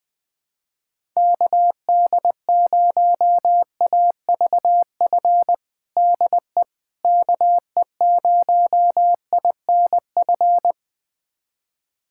Morse code, heard as a keyed tone, about 20 wpm, 700 Hz. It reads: KD0AVF DE KE0INF
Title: USA Callsign DE Callsign 20wpm